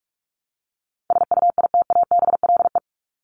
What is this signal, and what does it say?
Morse code, heard as a keyed tone, about 44 wpm, 700 Hz. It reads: SUITABLE
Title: Top 1251-1500 Words 45wpm